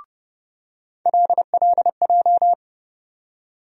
Morse code, heard as a keyed tone, about 30 wpm, 700 Hz. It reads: LLJ
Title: USA Call Sign Postfixes - 3 Letters 30wpm